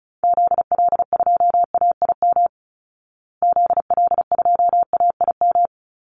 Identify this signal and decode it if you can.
Morse code, heard as a keyed tone, about 34 wpm, 700 Hz. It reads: ZL2AIM ZL2AIM
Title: Call Signs Repeated 2x - 35wpm